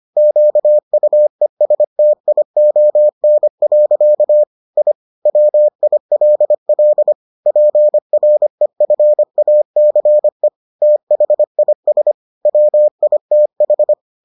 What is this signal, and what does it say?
Morse code, heard as a keyed tone, about 25 wpm, 600 Hz. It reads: QUESTION. I WILL PREFACE THIS WITH